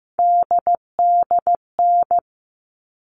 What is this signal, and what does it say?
Morse code, heard as a keyed tone, about 15 wpm, 700 Hz. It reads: DDN
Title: All Cut Numbers - 3-Digit 15wpm